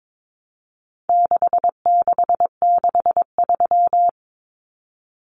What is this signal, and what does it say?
Morse code, heard as a keyed tone, about 22 wpm, 700 Hz. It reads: 6663
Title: Numbers - 4-Digit 22wpm